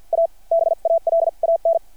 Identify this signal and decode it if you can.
Morse code, heard as a keyed tone, about 52 wpm, spaced as for 43 wpm, 660 Hz. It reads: A BALAN
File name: archived / CRSnet